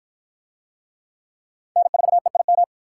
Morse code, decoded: N4EIM

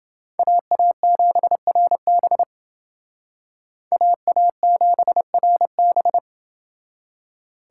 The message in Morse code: AA7RB AA7RB

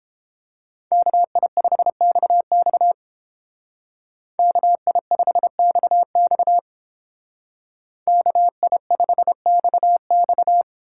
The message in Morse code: KI5XX KI5XX KI5XX

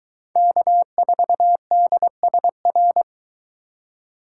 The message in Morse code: K4DSR

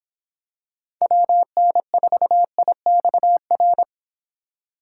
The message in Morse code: WN4IXR

